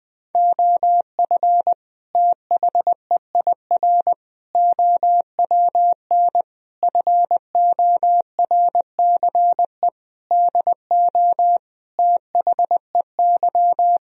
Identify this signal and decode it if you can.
Morse code, heard as a keyed tone, about 20 wpm, 700 Hz. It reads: OF THEIR OWN FORCE DO THEY
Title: Sentences from Top 1000 Words - Review Part 3 - 20wpm